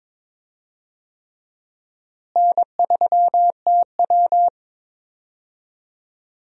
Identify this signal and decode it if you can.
Morse code, heard as a keyed tone, about 21 wpm, 700 Hz. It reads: N3TW
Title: USA Call Signs - 4 Characters - Repeated 3x - 22wpm